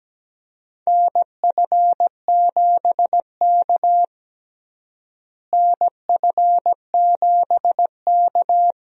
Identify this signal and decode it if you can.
Morse code, heard as a keyed tone, about 17 wpm, 700 Hz. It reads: NF7K NF7K
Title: USA Call Signs - 4 Characters - Repeated 2x - 17wpm